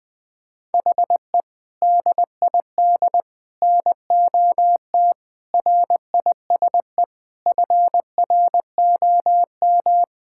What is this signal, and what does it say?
Morse code, heard as a keyed tone, about 19 wpm, 700 Hz. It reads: HE DID NOT RISE FROM
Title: Sentences from Top 1000 Words - Review Part 6 - 20wpm